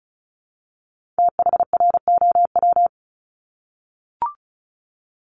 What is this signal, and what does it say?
Morse code, heard as a keyed tone, about 36 wpm, 700 Hz. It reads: THROW
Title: Multi-letter Phonemes 35wpm